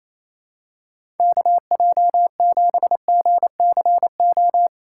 Morse code, heard as a keyed tone, about 28 wpm, 700 Hz. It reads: KJ7GCO
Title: USA Call Signs - 6 Characters - 4x Character Spacing - 28wpm